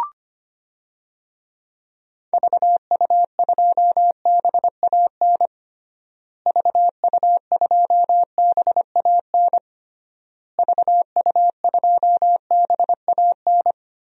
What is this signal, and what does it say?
Morse code, heard as a keyed tone, about 25 wpm, 700 Hz. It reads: VU2BAN VU2BAN VU2BAN
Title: DXCC-100 Call Signs - 6 Characters - Repeated 3x - 25wpm